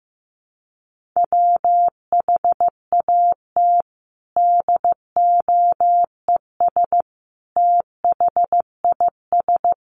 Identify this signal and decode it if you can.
Morse code, heard as a keyed tone, about 15 wpm, 700 Hz. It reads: WHAT DOES THIS
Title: Sentences from Top 1200 Words 15wpm